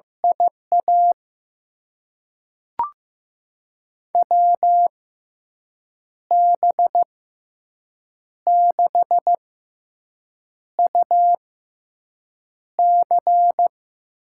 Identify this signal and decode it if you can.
Morse code, heard as a keyed tone, about 15 wpm, spaced as for 6 wpm, 700 Hz. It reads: IA WB6UC
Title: USA Call Signs - 6 Characters - 6x Character Spacing - 15wpm